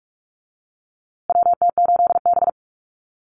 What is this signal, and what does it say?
Morse code, heard as a keyed tone, about 44 wpm, 700 Hz. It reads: WT8B
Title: USA Call Signs Repeated 3x 45wpm